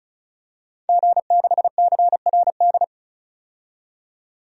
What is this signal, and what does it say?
Morse code, heard as a keyed tone, about 36 wpm, 700 Hz. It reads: G6CRD
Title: DXCC-100 Call Signs - 5 Characters - 35wpm